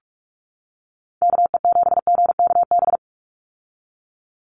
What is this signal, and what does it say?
Morse code, heard as a keyed tone, about 46 wpm, 700 Hz. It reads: KE7GKB
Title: USA Call Signs 45wpm